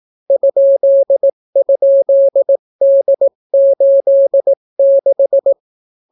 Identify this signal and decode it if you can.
Morse code, heard as a keyed tone, about 18 wpm, 550 Hz. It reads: ??D86